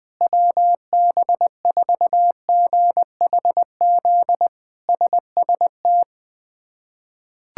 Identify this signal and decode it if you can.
Morse code, heard as a keyed tone, about 20 wpm, 700 Hz. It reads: WB4GHZ SST
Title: K1USN SST QSO Practice - 20wpm